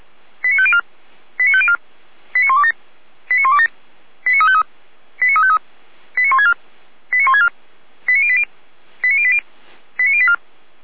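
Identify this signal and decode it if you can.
Five-tone selective call (ZVEI), heard as quick runs of five tones, tones of about 70 ms. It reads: 89504 89504 89127 89127 89353 89353 89164 89164 89080 89080 89084
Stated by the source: Received by SDR.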